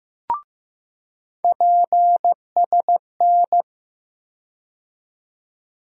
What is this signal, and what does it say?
Morse code, heard as a keyed tone, about 15 wpm, 700 Hz. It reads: PSN
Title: USA Call Sign Postfixes - 3 Letters 15wpm